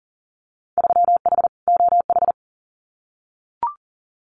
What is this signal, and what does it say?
Morse code, heard as a keyed tone, about 40 wpm, 700 Hz. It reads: 3H OH
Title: Winter Field Day Exchanges Encore 40wpm